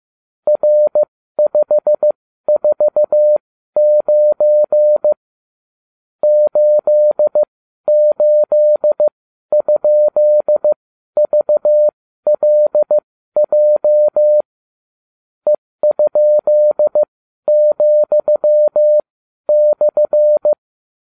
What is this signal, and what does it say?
Morse code, heard as a keyed tone, about 15 wpm, spaced as for 12 wpm, 600 Hz. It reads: R549 88?VLJ E?,/